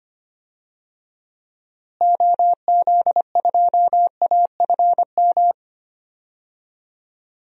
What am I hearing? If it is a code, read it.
Morse code, heard as a keyed tone, about 25 wpm, 700 Hz. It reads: OZ2AFM
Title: DXCC-100 Call Signs - 6 Characters - Repeated 2x - 25wpm